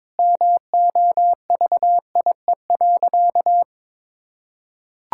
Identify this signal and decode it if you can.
Morse code, heard as a keyed tone, about 23 wpm, 700 Hz. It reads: MOVIE.
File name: practice / 22wpm / Sentences with 2-Word Phrases - 22wpm